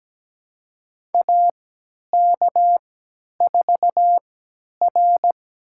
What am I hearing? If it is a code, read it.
Morse code, heard as a keyed tone, about 17 wpm, spaced as for 10 wpm, 700 Hz. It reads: AK4R
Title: USA Call Signs - 4 Characters - 3x Character Spacing - 17wpm